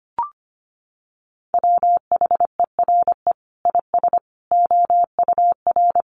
Morse code, heard as a keyed tone, about 25 wpm, 700 Hz. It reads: WHERE IS OUR
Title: Sentences from Top 1300 Words 25wpm